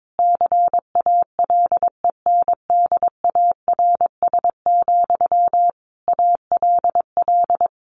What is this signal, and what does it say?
Morse code, heard as a keyed tone, about 22 wpm, 700 Hz. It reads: CALENDARS, ALL
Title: A Sound Of Thunder 22wpm